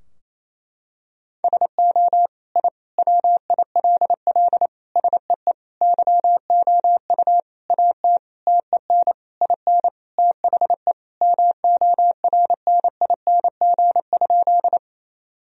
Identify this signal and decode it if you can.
Morse code, heard as a keyed tone, about 28 wpm, 700 Hz. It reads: SO I WILL SEE YOU AT TEN IN THE MORNING?
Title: Sentences from Top 1000 Words - Review Part 2 - 28wpm